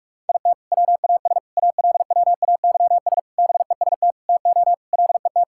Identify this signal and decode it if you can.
Morse code, heard as a keyed tone, about 47 wpm, 700 Hz. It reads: IT WAS ALWAYS BEST TO LET
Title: Sentences from Top 1000 Words - Review Part 6 - 45wpm